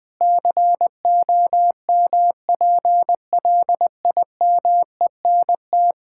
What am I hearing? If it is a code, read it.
Morse code, heard as a keyed tone, about 20 wpm, 700 Hz. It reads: COMPLIMENT
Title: Sentences with 8-Word Phrases - Spoken Beginning - 20wpm